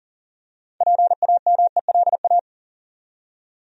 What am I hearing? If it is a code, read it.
Morse code, heard as a keyed tone, about 39 wpm, 700 Hz. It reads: PAMELA
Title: USA Common Names - 40wpm